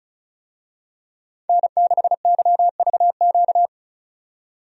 Morse code, heard as a keyed tone, about 36 wpm, 700 Hz. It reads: N6YVQ
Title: USA Call Signs - 5 Characters - 35wpm